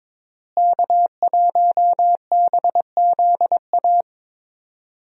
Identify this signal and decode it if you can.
Morse code, heard as a keyed tone, about 22 wpm, 700 Hz. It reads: K1BZA
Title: USA Call Signs - 5 Characters - Repeated 3x - 22wpm